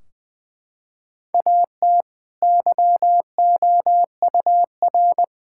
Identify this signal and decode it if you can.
Morse code, heard as a keyed tone, about 20 wpm, 700 Hz. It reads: AT YOUR
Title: Sentences from Top 1600 Words 20wpm